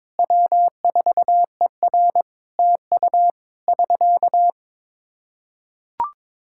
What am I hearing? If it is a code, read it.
Morse code, heard as a keyed tone, about 22 wpm, 700 Hz. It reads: W4ER TU <SK>
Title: Practice QSO - Exchange 4 - Ending - 22wpm